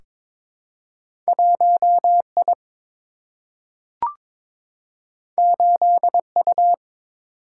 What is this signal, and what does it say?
Morse code, heard as a keyed tone, about 21 wpm, 700 Hz. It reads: 1 I 8 U
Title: Single-Letter Single-Number 22wpm